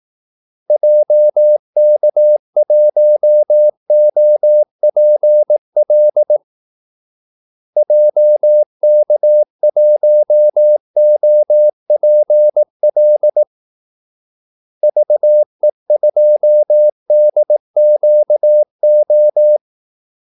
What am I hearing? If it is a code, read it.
Morse code, heard as a keyed tone, about 18 wpm, 600 Hz. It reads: JK1OPL JK1OPL VE2DQO
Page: Kaldesignaler 18 wpm